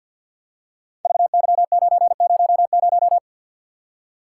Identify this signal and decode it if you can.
Morse code, heard as a keyed tone, about 52 wpm, 700 Hz. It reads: VY900
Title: Code Groups - 5 Letter:Numbers - 50wpm